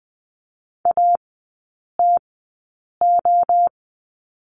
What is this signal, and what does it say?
Morse code, heard as a keyed tone, about 20 wpm, spaced as for 8 wpm, 690 Hz. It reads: ATO
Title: 145 Lesson 24e - Letter y - Three Characters 20wpm